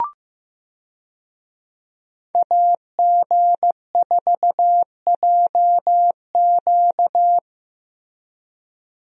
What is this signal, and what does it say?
Morse code, heard as a keyed tone, about 15 wpm, 700 Hz. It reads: AG4JQ
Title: USA Call Signs - 5 Characters - Repeated 2x - 15wpm